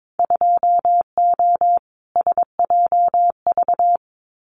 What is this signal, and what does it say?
Morse code, heard as a keyed tone, about 22 wpm, 700 Hz. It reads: 2O SJV
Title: Winter Field Day Exchanges Encore 22wpm